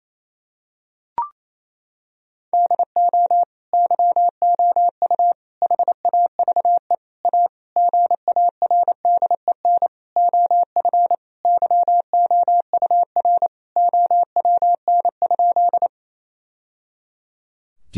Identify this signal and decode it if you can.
Morse code, heard as a keyed tone, about 28 wpm, 700 Hz. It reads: DO YOU HAVE A GARDEN OF YOUR OWN?
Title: Sentences from Top 1000 Words - Review Part 4 - 28wpm